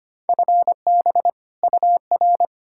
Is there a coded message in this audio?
Yes: Morse code from 0.3 to 2.5 s, heard as a keyed tone, about 33 wpm, 700 Hz. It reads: FB UR